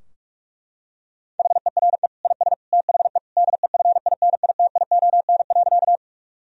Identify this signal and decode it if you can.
Morse code, heard as a keyed tone, about 41 wpm, 700 Hz. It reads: HERE IS THE DEFINITION.
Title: Sentences from Top 1000 Words - Review Part 4 - 45wpm